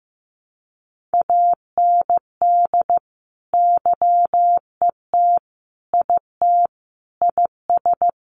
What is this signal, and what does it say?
Morse code, heard as a keyed tone, about 15 wpm, 700 Hz. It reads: AND YET IT IS
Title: Sentences from Top 1400 Words 15wpm